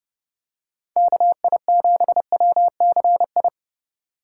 Morse code, heard as a keyed tone, about 30 wpm, 700 Hz. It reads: KI7WCI